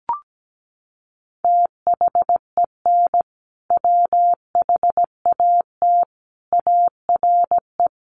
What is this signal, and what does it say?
Morse code, heard as a keyed tone, about 17 wpm, 700 Hz. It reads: THEN WHAT ARE
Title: Sentences from Top 100 Words 17wpm